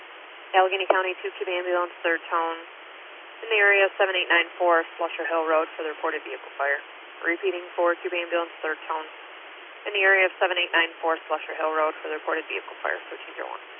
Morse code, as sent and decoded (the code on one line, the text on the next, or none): none